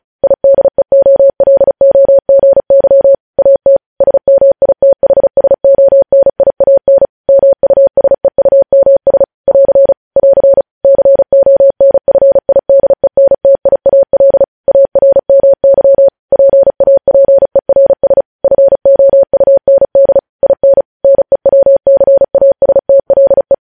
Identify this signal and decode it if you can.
Morse code, heard as a keyed tone, about 36 wpm, 550 Hz. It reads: IDEOLOGY AT SMITHSONIAN MUSEUMS <AR> <AR> CONFIDENTIAL ARMY PAPERS FOUND IN NEWCASTLE
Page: News headlines in Morse code at 35 wpm